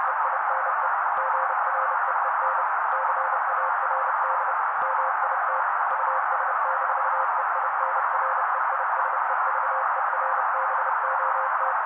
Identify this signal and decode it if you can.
Morse code, heard as a keyed tone, about 30 wpm, 530 Hz. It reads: EINE GREEN CARD MIT US-EINREISEVERBOT
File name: daily / 29wpm noise